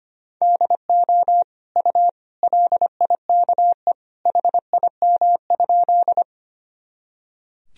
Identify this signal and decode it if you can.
Morse code, heard as a keyed tone, about 25 wpm, 700 Hz. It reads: DO U LIKE HIM?
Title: Sentences from Top 100 Words and Abbreviations 25wpm